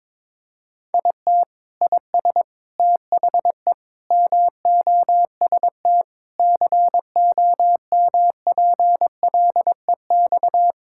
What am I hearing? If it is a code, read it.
Morse code, heard as a keyed tone, about 21 wpm, 700 Hz. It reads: IT IS THE MOST COMPLEX
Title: Sentences from Top 1000 Words - Review Part 6 - 22wpm